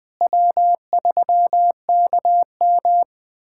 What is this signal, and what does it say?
Morse code, heard as a keyed tone, about 20 wpm, 700 Hz. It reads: W3KM
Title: USA Callsign DE Callsign 20wpm